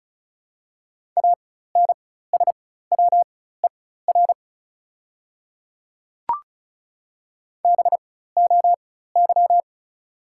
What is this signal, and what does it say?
Morse code, heard as a keyed tone, about 36 wpm, spaced as for 16 wpm, 690 Hz. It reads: ANSWER BOY
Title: 4x Character Spacing - Top 500 Words - 35wpm